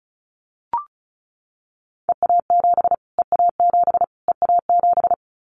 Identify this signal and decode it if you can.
Morse code, heard as a keyed tone, about 34 wpm, 700 Hz. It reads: EA7 EA7 EA7
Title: Callsign Prefixes for Top 25 DXCC - Repeated 3x - 35wpm